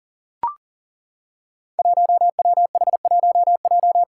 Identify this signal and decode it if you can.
Morse code, heard as a keyed tone, about 40 wpm, 700 Hz. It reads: 1WH1J
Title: Five Challenging Letter-Number Combinations 40wpm